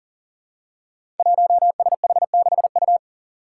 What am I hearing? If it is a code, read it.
Morse code, heard as a keyed tone, about 40 wpm, 700 Hz. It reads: 1SH6U
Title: Five Challenging Letter-Number Combinations 40wpm